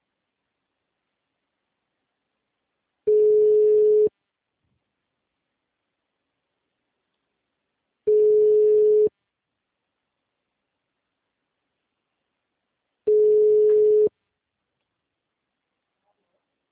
Ringback tone (Europe, 425 Hz).
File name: out- -2049- - - .41411